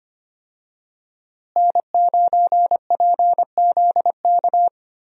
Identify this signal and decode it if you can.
Morse code, heard as a keyed tone, about 25 wpm, 700 Hz. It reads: N9PZK
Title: USA Call Signs - 5 Characters - Repeated 3x - 25wpm